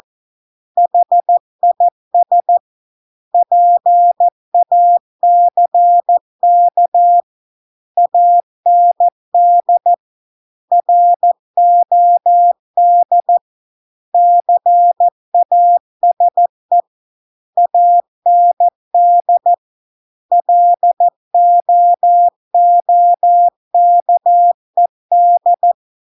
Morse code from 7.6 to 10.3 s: AND